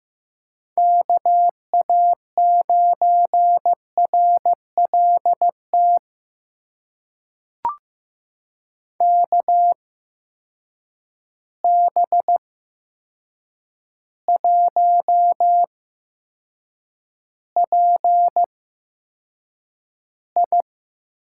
Morse code, heard as a keyed tone, about 15 wpm, spaced as for 7 wpm, 700 Hz. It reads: KA9RLT KB1PI
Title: USA Call Signs - 6 Characters - 8x Character Spacing - 15wpm